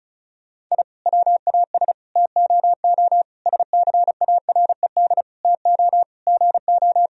Morse code, heard as a keyed tone, about 36 wpm, 700 Hz. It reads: I WAS TOO SCARED TO GO
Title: Sentences from Top 1000 Words - Review Part 4 - 35wpm